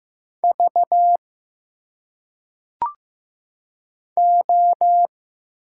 Morse code, heard as a keyed tone, about 15 wpm, 700 Hz. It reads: V O
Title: Single Letters - Phonetic Pronunciation 15wpm